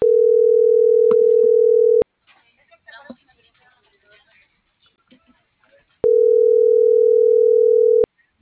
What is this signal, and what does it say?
Ringback tone (North America), 440 + 480 Hz.